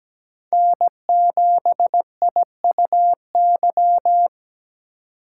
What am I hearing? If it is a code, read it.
Morse code, heard as a keyed tone, about 20 wpm, 700 Hz. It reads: N7IUY
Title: Speed Racing - USA 1x3 Call Signs 17wpm